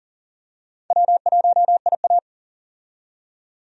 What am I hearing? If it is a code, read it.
Morse code, heard as a keyed tone, about 40 wpm, 700 Hz. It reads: W1IA
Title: USA Call Signs - 4 Characters - 40wpm